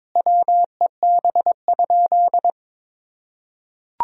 Morse code, heard as a keyed tone, about 22 wpm, 700 Hz. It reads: WEB?
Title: Sentences from Top 1000 Words - Review Part 4 - 22wpm